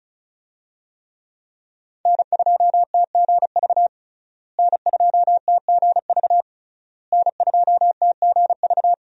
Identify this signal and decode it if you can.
Morse code, heard as a keyed tone, about 34 wpm, 700 Hz. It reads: N2TGV N2TGV N2TGV
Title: USA Call Signs - 5 Characters - Repeated 3x - 35wpm